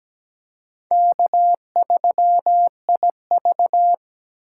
Morse code, heard as a keyed tone, about 21 wpm, 700 Hz. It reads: K3IV
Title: Speed Racing - USA 1x2 Call Signs 17wpm